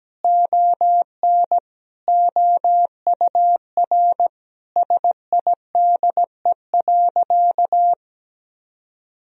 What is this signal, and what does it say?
Morse code, heard as a keyed tone, about 17 wpm, 700 Hz. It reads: ON OUR SIDE.
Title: Sentences from Top 1000 Words - Review Part 3 - 17wpm